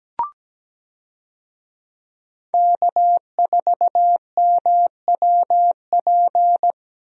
Morse code, heard as a keyed tone, about 17 wpm, 700 Hz. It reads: K4MWP
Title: USA Call Signs - 5 Characters - Repeated 3x - 17wpm